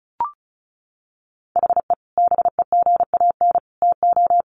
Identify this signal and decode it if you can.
Morse code, heard as a keyed tone, about 34 wpm, 700 Hz. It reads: HE BEGAN TO
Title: Sentences from Top 500 Words 35wpm